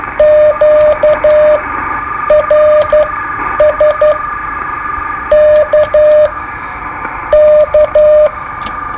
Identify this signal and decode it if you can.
Morse code, heard as a keyed tone, about 11 wpm, spaced as for 8 wpm, 610 Hz, with radio noise. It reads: QRS K K
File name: qso-01